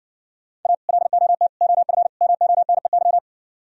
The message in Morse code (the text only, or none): A LOT OF MONEY